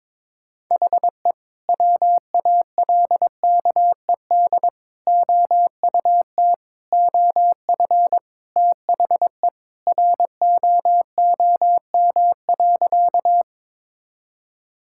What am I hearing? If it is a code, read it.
Morse code, heard as a keyed tone, about 23 wpm, 700 Hz. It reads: HE WALKED OUT OF THE ROOM.
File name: practice / 22wpm / Sentences with 2-Word Phrases - 22wpm